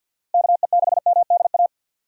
Morse code, heard as a keyed tone, about 47 wpm, 700 Hz. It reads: KE6MDA
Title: USA Call Signs - 6 Characters - 50wpm